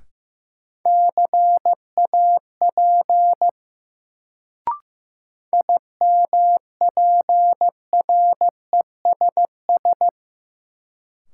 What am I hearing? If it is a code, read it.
Morse code, heard as a keyed tone, about 15 wpm, 700 Hz. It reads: CAP IMPRESS
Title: Sentences from Top 1600 Words - New Word Drill - 15wpm